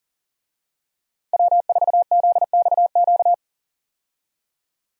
Morse code, heard as a keyed tone, about 40 wpm, 700 Hz. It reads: W4ZXQ
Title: USA Call Signs - 5 Characters - 40wpm